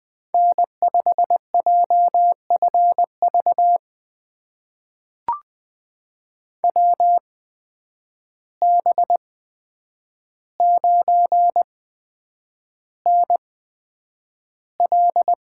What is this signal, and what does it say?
Morse code, heard as a keyed tone, about 20 wpm, spaced as for 9 wpm, 700 Hz. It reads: N5JFV WB9NL
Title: USA Call Signs - 8x Character Spacing - 20wpm